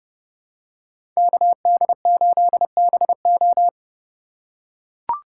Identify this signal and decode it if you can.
Morse code, heard as a keyed tone, about 30 wpm, 700 Hz. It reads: KD8BO
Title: USA Call Signs - 4x Character Spacing - 30wpm